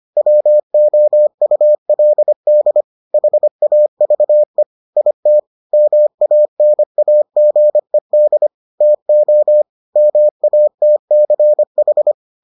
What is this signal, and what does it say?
Morse code, heard as a keyed tone, about 25 wpm, 600 Hz. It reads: WOULD HAVE IT MANAGED TO MATCH